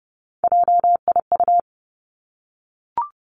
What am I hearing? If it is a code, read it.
Morse code, heard as a keyed tone, about 32 wpm, 700 Hz. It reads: JIU
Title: Three Challenging Letter-Number Combinations 30wpm